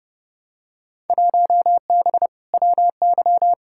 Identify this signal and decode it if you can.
Morse code, heard as a keyed tone, about 29 wpm, 700 Hz. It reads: 1B WY
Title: Field Day Exchanges Encore - 30wpm